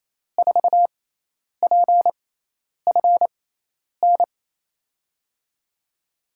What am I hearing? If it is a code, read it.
Morse code, heard as a keyed tone, about 27 wpm, spaced as for 10 wpm, 700 Hz. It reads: 4PFN
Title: USA Call Signs - 6 Characters - 6x Character Spacing - 28wpm